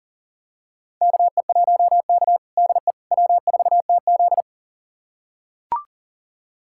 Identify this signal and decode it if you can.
Morse code, heard as a keyed tone, about 39 wpm, 700 Hz. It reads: KE1K DE W4TZ
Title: USA Callsign DE Callsign - 4 Characters - 40wpm